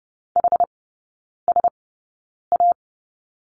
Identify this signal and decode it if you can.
Morse code, heard as a keyed tone, about 30 wpm, spaced as for 9 wpm, 700 Hz. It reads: HSA
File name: course / 30:10wpm / 087 Lesson 15e - Number 5 - Three Characters 30wpm